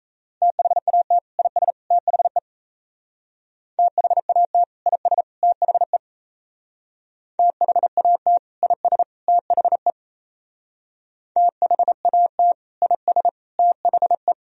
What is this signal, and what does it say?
Morse code, heard as a keyed tone, about 37 wpm, 700 Hz. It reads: THAT IS THE THAT IS THE THAT IS THE THAT IS THE